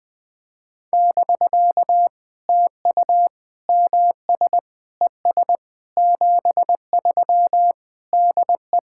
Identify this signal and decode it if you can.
Morse code, heard as a keyed tone, about 20 wpm, 700 Hz. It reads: <BK> TU MS ES 73 DE
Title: POTA QSO Practice - 20wpm